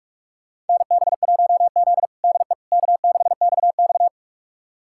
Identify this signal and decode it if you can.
Morse code, heard as a keyed tone, about 46 wpm, 700 Hz. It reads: NB1Z DE K6XX
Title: USA Callsign DE Callsign - 4 Characters - 45wpm